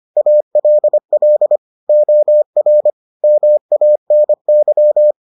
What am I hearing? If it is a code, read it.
Morse code, heard as a keyed tone, about 25 wpm, 600 Hz. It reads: ALL OR MANY